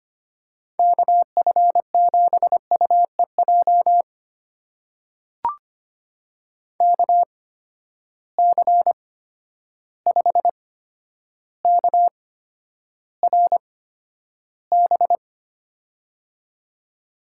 Morse code, heard as a keyed tone, about 25 wpm, spaced as for 11 wpm, 700 Hz. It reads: KF7UEJ KC5KRB
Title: USA Call Signs - 6 Characters - 8x Character Spacing - 25wpm